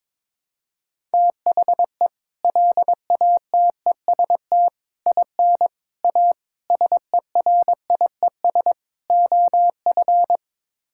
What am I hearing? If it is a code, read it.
Morse code, heard as a keyed tone, about 22 wpm, 700 Hz. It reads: THE LATEST IN A SERIES OF
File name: practice / 22wpm / Sets of 6 Words - 22wpm